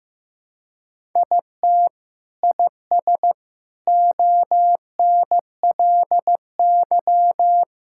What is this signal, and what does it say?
Morse code, heard as a keyed tone, about 15 wpm, 700 Hz. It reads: IT IS ONLY
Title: Sentences from Top 1000 Words - Review Part 5 - 15wpm